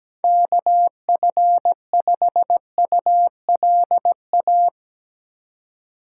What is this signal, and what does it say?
Morse code, heard as a keyed tone, about 17 wpm, 700 Hz. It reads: KF5ULA
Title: USA Call Signs - 6 Characters - 2x Character Spacing - 17wpm